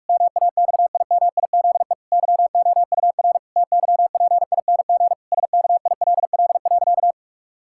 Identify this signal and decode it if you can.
Morse code, heard as a keyed tone, about 46 wpm, 700 Hz. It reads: MAXIMIZE YOUR TYPING SKILL.
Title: Pangram Challenge v2 - 45wpm